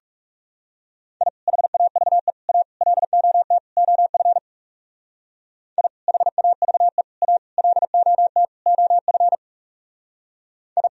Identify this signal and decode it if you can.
Morse code, heard as a keyed tone, about 36 wpm, 700 Hz. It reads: I HAVE A LOT OF I HAVE A LOT OF I
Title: Speed-Racing - Sets of 5 Words 30wpm